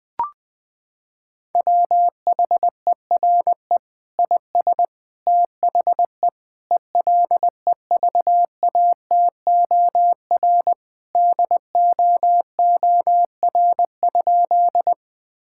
Morse code, heard as a keyed tone, about 20 wpm, 700 Hz. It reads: WHERE IS THE ELEVATOR DOOR?
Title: Sentences from Top 1600 Words 20wpm